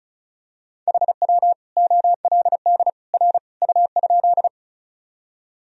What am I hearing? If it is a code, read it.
Morse code, heard as a keyed tone, about 36 wpm, 700 Hz. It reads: HW OLD R U?
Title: Sentences from Top 200 Words and Abbreviations 35wpm